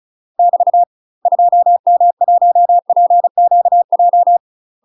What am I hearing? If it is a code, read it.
Morse code, heard as a keyed tone, about 36 wpm, 700 Hz. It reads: <BT> 2M1PQJ
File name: UK MASTER QSOs TRAINING / 35WPM / UK MASTER QSOs 0010